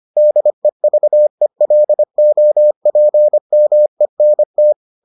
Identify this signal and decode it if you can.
Morse code, heard as a keyed tone, about 25 wpm, 600 Hz. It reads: DEVELOPMENT